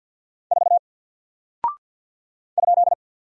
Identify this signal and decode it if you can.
Morse code, heard as a keyed tone, about 48 wpm, 700 Hz. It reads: 4 ?
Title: Single Character (Letter:Number:Punctuation:Prosign) 50wpm